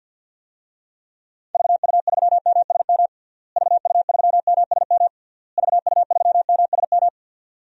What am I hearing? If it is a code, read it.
Morse code, heard as a keyed tone, about 51 wpm, 700 Hz. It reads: VU3MSM VU3MSM VU3MSM
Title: DXCC-100 Call Signs - 6 Characters - Repeated 3x - 50wpm